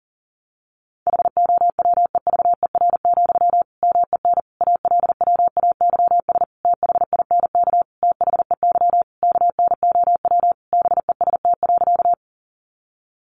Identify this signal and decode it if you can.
Morse code, heard as a keyed tone, about 40 wpm, 700 Hz. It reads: HOWEVER, MEN ALWAYS THINK THEY KNOW BEST.
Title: Sentences from Top 600 Words 40wpm